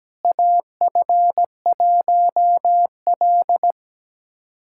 Morse code, heard as a keyed tone, about 17 wpm, 700 Hz. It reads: AF1L
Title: USA Callsign DE Callsign - 4 Characters - 17wpm